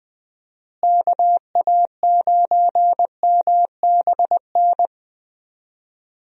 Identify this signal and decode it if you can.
Morse code, heard as a keyed tone, about 20 wpm, 700 Hz. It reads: KA9MBN
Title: USA Call Signs - 6 Characters - Repeated 2x - 20wpm